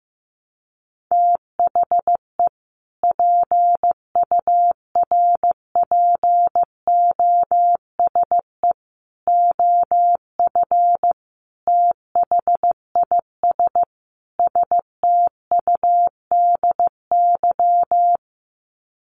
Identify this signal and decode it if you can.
Morse code, heard as a keyed tone, about 15 wpm, 700 Hz. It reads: THE PURPOSE OF THIS STUDY
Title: Sets of 5 Words 15wpm